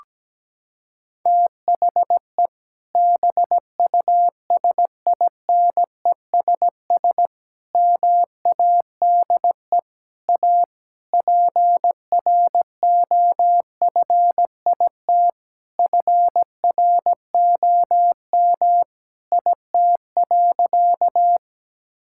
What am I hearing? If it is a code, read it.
Morse code, heard as a keyed tone, about 17 wpm, 700 Hz. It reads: THE BUSINESS MADE A PROFIT FROM IT.
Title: Sentences from Top 1000 Words - Review Part 5 - 17wpm